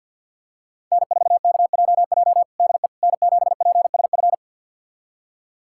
Morse code, heard as a keyed tone, about 48 wpm, 700 Hz. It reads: N4KJJ DE N7PSF
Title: USA Callsign DE Callsign - 5 Characters - 50wpm